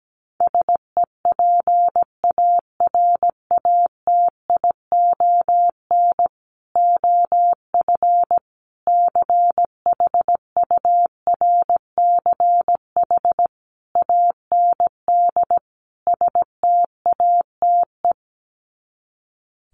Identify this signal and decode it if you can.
Morse code, heard as a keyed tone, about 17 wpm, 700 Hz. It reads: SEPARATION OF CHURCH AND STATE
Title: Sets of 5 Words 17wpm